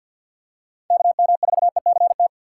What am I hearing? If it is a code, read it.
Morse code, heard as a keyed tone, about 47 wpm, 700 Hz. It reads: KM4ECT
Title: USA Call Signs - 6 Characters - 4x Character Spacing - 50wpm